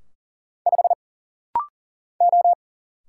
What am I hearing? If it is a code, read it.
Morse code, heard as a keyed tone, about 40 wpm, 700 Hz. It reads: 5 O
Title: Single Letter-Number - Rapid-Fire 40wpm